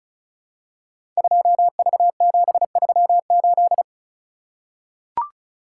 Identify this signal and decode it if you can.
Morse code, heard as a keyed tone, about 36 wpm, 700 Hz. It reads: 2V738
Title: Code Groups - 5 Letter:Numbers - 35wpm